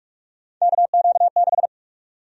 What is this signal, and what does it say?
Morse code, heard as a keyed tone, about 44 wpm, 700 Hz. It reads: KQ6
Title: USA Call Sign Prefixes - Repeated 2x - 45wpm